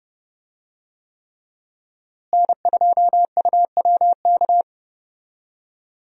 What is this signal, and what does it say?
Morse code, heard as a keyed tone, about 30 wpm, 700 Hz. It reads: N2UWK
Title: USA Call Signs - 5 Characters - Repeated 3x - 30wpm